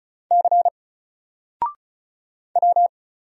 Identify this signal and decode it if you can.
Morse code, heard as a keyed tone, about 36 wpm, 700 Hz. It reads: C W
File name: Single-Letter-Number-35wpm